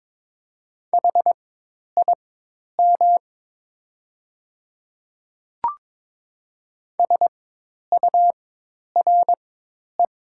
Morse code, heard as a keyed tone, about 22 wpm, spaced as for 10 wpm, 700 Hz. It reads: HIM SURE
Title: 4x Character Spacing - Top 300 Words - 22wpm